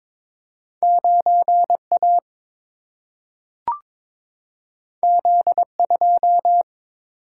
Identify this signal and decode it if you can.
Morse code, heard as a keyed tone, about 22 wpm, 700 Hz. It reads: 9A Z2
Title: Single-Letter Single-Number 22wpm